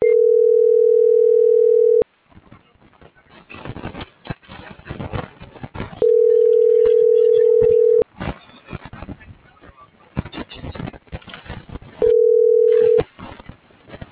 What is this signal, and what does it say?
Ringback tone (North America), 440 + 480 Hz.